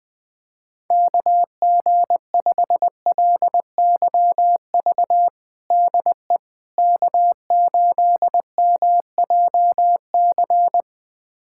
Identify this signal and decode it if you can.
Morse code, heard as a keyed tone, about 21 wpm, 700 Hz. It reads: KG5LYV DE K8MJC
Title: USA Callsign DE Callsign 20wpm